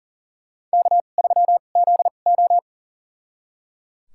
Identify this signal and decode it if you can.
Morse code, heard as a keyed tone, about 37 wpm, spaced as for 28 wpm, 700 Hz. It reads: K3ZO
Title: USA Call Signs - 4 Characters - 2x Character Spacing - 40wpm